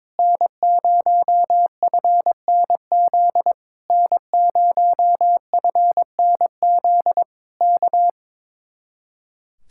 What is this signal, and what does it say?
Morse code, heard as a keyed tone, about 22 wpm, 700 Hz. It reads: N0FNZ N0FNZ K
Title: Speed-Racing Common QSO Phrases - 22wpm